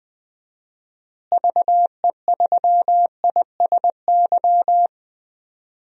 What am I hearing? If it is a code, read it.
Morse code, heard as a keyed tone, about 20 wpm, 700 Hz. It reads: VE3ISY
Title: DXCC-100 Call Signs - 6 Characters - Repeated 2x - 20wpm